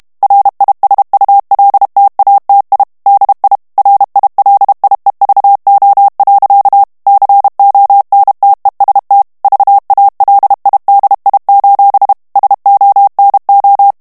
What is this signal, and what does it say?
Morse code, heard as a keyed tone, about 32 wpm, 800 Hz. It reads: RISULTATI DI RILIEVO. CONTEST VALIDI: SONO
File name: texte clair 2 vitesse 30 mots minute